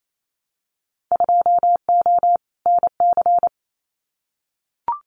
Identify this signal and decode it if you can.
Morse code, heard as a keyed tone, about 28 wpm, 700 Hz. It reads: 2O NC
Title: Winter Field Day Exchanges 28wpm